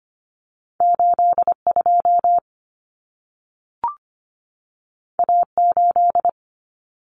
Morse code, heard as a keyed tone, about 25 wpm, 700 Hz. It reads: 82 A8
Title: Common Cut Numbers - 2-Digit 25wpm